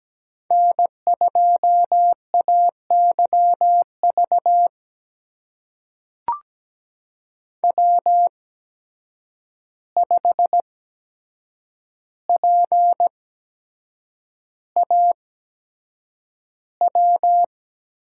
Morse code, heard as a keyed tone, about 17 wpm, spaced as for 7 wpm, 700 Hz. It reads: N2AYV W5PAW
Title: USA Call Signs - 5 Characters - 8x Character Spacing - 17wpm